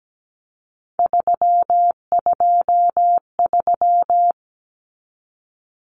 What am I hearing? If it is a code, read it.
Morse code, heard as a keyed tone, about 17 wpm, 700 Hz. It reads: 323
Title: Common Cut Numbers - 3-Digit 17wpm